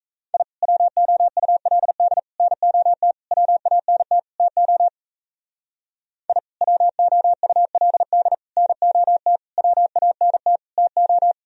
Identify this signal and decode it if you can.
Morse code, heard as a keyed tone, about 34 wpm, 700 Hz. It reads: I WOULD NOT WANT TO I WOULD NOT WANT TO
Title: Speed-Racing - Sets of 5 Words 28wpm